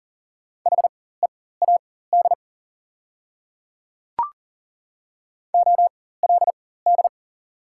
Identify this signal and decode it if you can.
Morse code, heard as a keyed tone, about 40 wpm, spaced as for 19 wpm, 690 Hz. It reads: HEAD OLD
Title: 4x Character Spacing - Top 300 Words - 40wpm